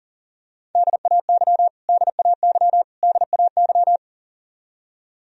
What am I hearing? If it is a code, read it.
Morse code, heard as a keyed tone, about 40 wpm, 700 Hz. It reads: DAY DAY DAY
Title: Top 100 Words Repeated 40wpm